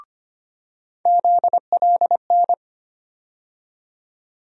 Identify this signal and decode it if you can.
Morse code, heard as a keyed tone, about 25 wpm, 700 Hz. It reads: ZLN
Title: USA Call Sign Postfixes - 3 Letters 25wpm